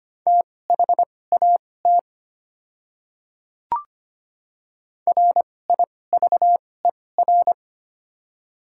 Morse code, heard as a keyed tone, about 25 wpm, spaced as for 18 wpm, 700 Hz. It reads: THAT RIVER
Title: 2x Character Spacing - Top 500 Words - 25wpm